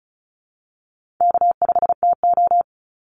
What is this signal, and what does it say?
Morse code, heard as a keyed tone, about 34 wpm, 700 Hz. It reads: K5TO
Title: Call Signs Repeated 3x - 35wpm